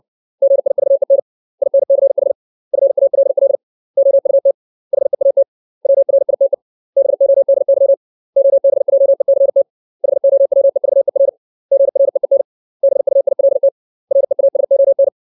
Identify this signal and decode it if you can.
Morse code, heard as a keyed tone, about 60 wpm, 550 Hz. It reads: GIVEN INCH FALL OUT HEAT WRITE BODY OBJECT HOWEVER GREEN BUILT REASON
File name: 632 words / 632 Words 60WPM